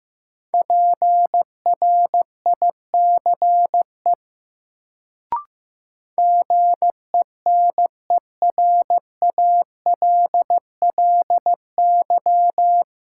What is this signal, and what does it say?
Morse code, heard as a keyed tone, about 15 wpm, 700 Hz. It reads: PRICE GENERALLY
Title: Sentences from Top 600 Words - New Word Drill - 15wpm